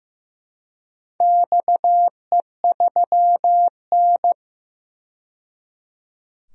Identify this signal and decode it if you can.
Morse code, heard as a keyed tone, about 15 wpm, 700 Hz. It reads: XE3N
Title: DXCC-100 Call Signs - 4 Characters - 15wpm